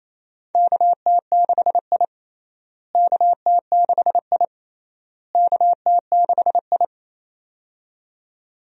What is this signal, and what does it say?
Morse code, heard as a keyed tone, about 28 wpm, 700 Hz. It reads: KT6I KT6I KT6I
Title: USA Call Signs - 4 Characters - Repeated 3x - 28wpm